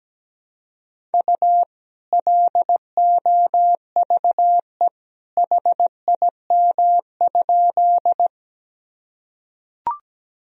Morse code, heard as a keyed tone, about 17 wpm, 700 Hz. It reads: U LOVE HIM?
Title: Sentences from Top 400 Words and Abbreviations 17wpm